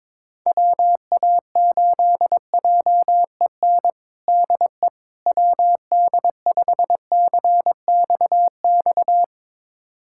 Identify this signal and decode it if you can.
Morse code, heard as a keyed tone, about 22 wpm, 700 Hz. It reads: WA8JEN DE WD5CXX
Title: USA Callsign DE Callsign 22wpm